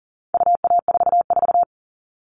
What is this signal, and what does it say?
Morse code, heard as a keyed tone, about 40 wpm, 700 Hz. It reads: UA44
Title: All Cut Numbers - 4-Digit 40wpm